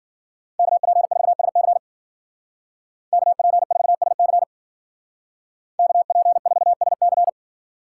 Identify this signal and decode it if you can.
Morse code, heard as a keyed tone, about 53 wpm, 700 Hz. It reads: KP4SC KP4SC KP4SC